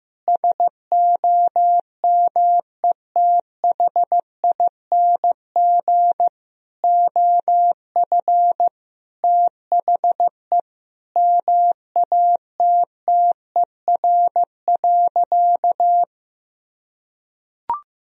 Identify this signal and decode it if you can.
Morse code, heard as a keyed tone, about 16 wpm, 700 Hz. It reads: SOMETHING OF THE MATTER.
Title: Sentences from Top 1000 Words - Review Part 3 - 15wpm